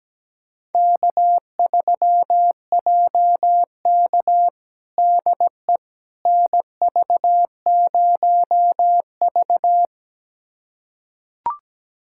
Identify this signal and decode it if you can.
Morse code, heard as a keyed tone, about 17 wpm, 700 Hz. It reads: K3JK DE NV0V
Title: USA Callsign DE Callsign - 4 Characters - 17wpm